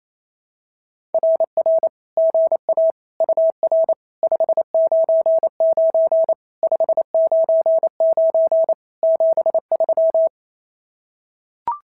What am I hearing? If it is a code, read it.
Morse code, heard as a keyed tone, about 28 wpm, 660 Hz. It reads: RR GA UR 599 599 73
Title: SOTA QSO Practice 28wpm